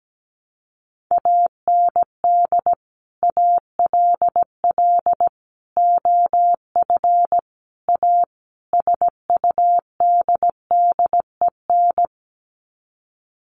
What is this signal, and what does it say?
Morse code, heard as a keyed tone, about 17 wpm, 700 Hz. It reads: AND ALL OF A SUDDEN
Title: Sets of 5 Words 17wpm